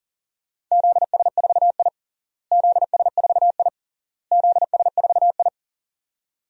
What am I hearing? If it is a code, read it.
Morse code, heard as a keyed tone, about 40 wpm, 700 Hz. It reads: ZS4I ZS4I ZS4I
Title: DXCC-100 Call Signs - 4 Characters - Repeated 3x - 40wpm